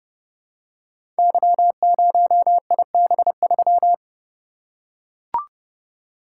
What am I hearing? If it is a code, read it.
Morse code, heard as a keyed tone, about 30 wpm, 700 Hz. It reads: Y0IB3
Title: Code Groups - 5 Letter:Numbers - 30wpm